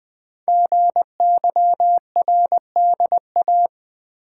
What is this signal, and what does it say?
Morse code, heard as a keyed tone, about 20 wpm, 700 Hz. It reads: GYRDA
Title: Code Groups - 5 Letters - 20wpm